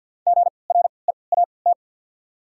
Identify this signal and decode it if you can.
Morse code, heard as a keyed tone, about 51 wpm, 700 Hz. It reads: G R E A T
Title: 3x Character Spacing - Top 300 Words - 50wpm